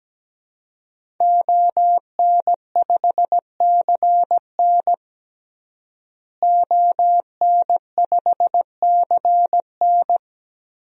Morse code, heard as a keyed tone, about 17 wpm, 700 Hz. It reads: ON5CN ON5CN
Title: DXCC-100 Call Signs - 5 Characters - Repeated 3x - 17wpm